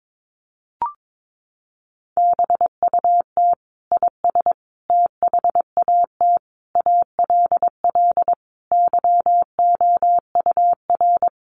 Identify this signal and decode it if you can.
Morse code, heard as a keyed tone, about 22 wpm, 700 Hz. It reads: BUT IS THAT ALL YOUR
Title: Sentences from Top 1300 Words 22wpm